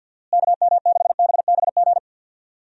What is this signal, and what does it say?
Morse code, heard as a keyed tone, about 47 wpm, 700 Hz. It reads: KM6BBG
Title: USA Call Signs - 6 Characters - Repeated 2x - 50wpm